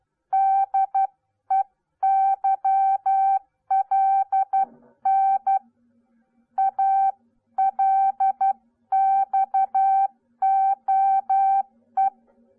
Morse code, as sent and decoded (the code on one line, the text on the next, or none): -.. -.-- .-.. -. / .- .-.. -..- --- .
DEYLN ALXOE